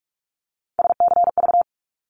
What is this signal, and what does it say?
Morse code, heard as a keyed tone, about 44 wpm, 700 Hz. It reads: SCV
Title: ARRL Sections 45wpm